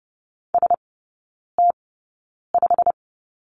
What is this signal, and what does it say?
Morse code, heard as a keyed tone, about 30 wpm, spaced as for 9 wpm, 700 Hz. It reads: ST5
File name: course / 30:10wpm / 087 Lesson 15e - Number 5 - Three Characters 30wpm